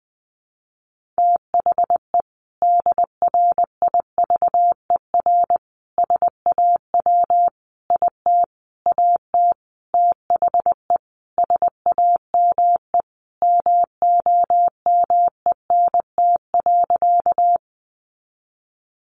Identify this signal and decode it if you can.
Morse code, heard as a keyed tone, about 20 wpm, 700 Hz. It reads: THE DRIVER SAW IT AT THE SAME MOMENT.
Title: Sentences from Top 1000 Words 20wpm